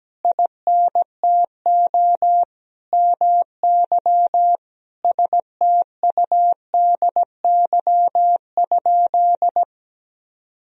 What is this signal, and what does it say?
Morse code, heard as a keyed tone, about 17 wpm, 700 Hz. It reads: INTO MY STUDY?
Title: Sentences from Top 1000 Words - Review Part 1 - 17wpm